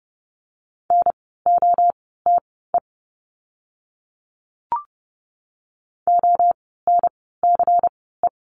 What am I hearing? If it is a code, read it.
Morse code, heard as a keyed tone, about 30 wpm, spaced as for 17 wpm, 700 Hz. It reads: NOTE ONCE
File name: Lesson 16f - Letter c - Words 30wpm